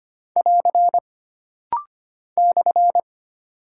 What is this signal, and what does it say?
Morse code, heard as a keyed tone, about 25 wpm, 700 Hz. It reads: <AR> /
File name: Single Punctuation-Prosign - Rapid-Fire - 25wpm